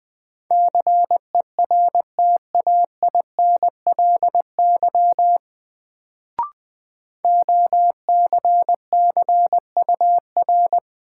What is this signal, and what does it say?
Morse code, heard as a keyed tone, about 20 wpm, 700 Hz. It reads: CERTAINLY OCCUR
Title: Sentences from Top 900 Words - New Word Drill - 20wpm